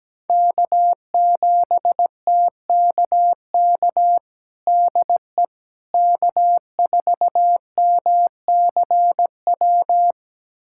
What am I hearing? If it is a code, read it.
Morse code, heard as a keyed tone, about 17 wpm, 700 Hz. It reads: K7TKK DE K4MCW
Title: USA Callsign DE Callsign - 5 Characters - 17wpm